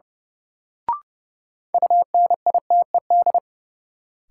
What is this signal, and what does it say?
Morse code, heard as a keyed tone, about 30 wpm, 700 Hz. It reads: UNITED
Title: Sets of 2 Words - New Word Drill - 30wpm